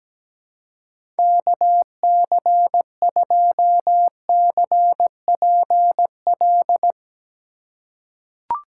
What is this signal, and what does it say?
Morse code, heard as a keyed tone, about 17 wpm, 700 Hz. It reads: KC2CPL
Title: USA Call Signs - 4x Character Spacing - 17wpm